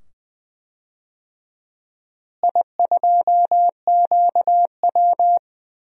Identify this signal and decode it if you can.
Morse code, heard as a keyed tone, about 20 wpm, 700 Hz. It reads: I2QW
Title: DXCC-100 Call Signs - 4 Characters - Repeated 2x - 20wpm